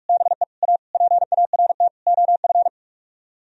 Morse code, heard as a keyed tone, about 49 wpm, 700 Hz. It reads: BE A PART OF